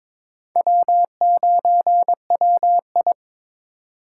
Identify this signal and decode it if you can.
Morse code, heard as a keyed tone, about 26 wpm, 700 Hz. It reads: W9WI
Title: Speed Racing - USA Call Signs 22wpm